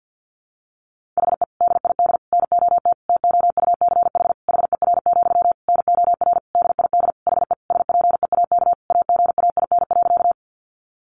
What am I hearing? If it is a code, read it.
Morse code, heard as a keyed tone, about 50 wpm, 700 Hz. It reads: HE DID NOT TOUCH HER, NOR DID HE SPEAK AGAIN.
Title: Sentences from Top 1500 Words 50wpm